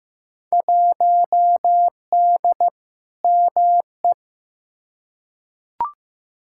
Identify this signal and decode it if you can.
Morse code, heard as a keyed tone, about 15 wpm, 700 Hz. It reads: 1D ME
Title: Field Day Exchanges - 15wpm